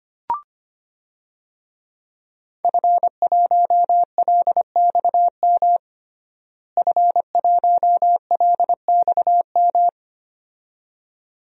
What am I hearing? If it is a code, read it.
Morse code, heard as a keyed tone, about 25 wpm, 700 Hz. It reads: F1LXM F1LXM
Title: DXCC-100 Call Signs - 5 Characters - Repeated 2x - 25wpm